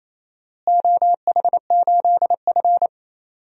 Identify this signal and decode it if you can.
Morse code, heard as a keyed tone, about 27 wpm, 700 Hz. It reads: OH8F
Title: DXCC-100 Call Signs - 4 Characters - Repeated 3x - 28wpm